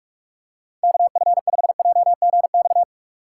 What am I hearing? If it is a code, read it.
Morse code, heard as a keyed tone, about 46 wpm, 700 Hz. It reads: KF5JGX
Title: USA Call Signs - 6 Characters - 3x Character Spacing - 45wpm